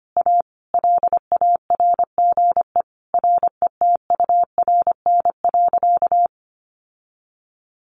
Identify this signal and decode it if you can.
Morse code, heard as a keyed tone, about 25 wpm, 700 Hz. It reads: A LARGE RETURN.
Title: Sentences from Top 1100 Words 25wpm